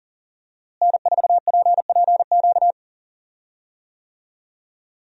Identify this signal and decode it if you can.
Morse code, heard as a keyed tone, about 37 wpm, 700 Hz. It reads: N4PPQ
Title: USA Call Signs - 5 Characters - Repeated 3x - 40wpm